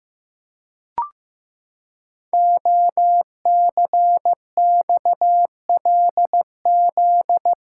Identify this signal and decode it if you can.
Morse code, heard as a keyed tone, about 15 wpm, 700 Hz. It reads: OCXLZ
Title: Code Groups - 5 Letters - 15wpm